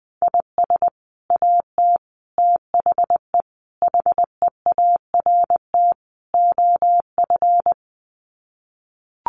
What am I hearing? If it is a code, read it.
Morse code, heard as a keyed tone, about 20 wpm, 700 Hz. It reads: IS AT THE HEART OF
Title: Sets of 5 Words Encore2 20wpm